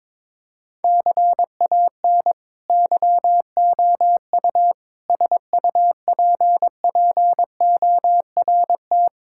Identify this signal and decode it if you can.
Morse code, heard as a keyed tone, about 22 wpm, 700 Hz. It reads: CAN YOU SUPPORT
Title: Sentences from Top 1000 Words - Review Part 5 - 22wpm